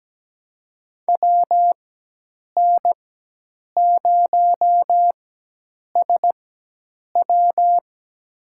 Morse code, heard as a keyed tone, about 17 wpm, spaced as for 8 wpm, 700 Hz. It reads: WN0SW
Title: USA Call Signs - 5 Characters - 4x Character Spacing - 17wpm